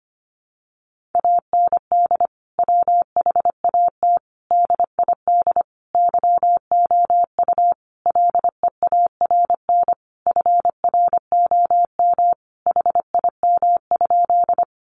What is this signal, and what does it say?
Morse code, heard as a keyed tone, about 25 wpm, 700 Hz. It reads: AND WHAT DID YOU LEARN FROM HIM?
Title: Sentences from Top 300 Words 25wpm